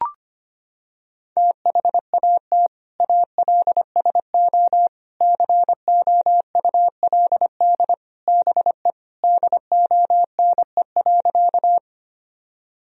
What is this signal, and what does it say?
Morse code, heard as a keyed tone, about 25 wpm, 700 Hz. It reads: THAT ALSO COULD BE DONE.
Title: Sentences from Top 1000 Words - Review Part 1 - 25wpm